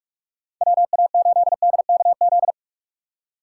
Morse code, heard as a keyed tone, about 46 wpm, 700 Hz. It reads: WA8DKZ
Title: USA Call Signs - 6 Characters - 6x Character Spacing - 45wpm